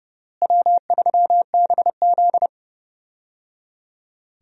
Morse code, heard as a keyed tone, about 30 wpm, 700 Hz. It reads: W3BZ
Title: USA Call Signs - 4 Characters - Repeated 3x - 30wpm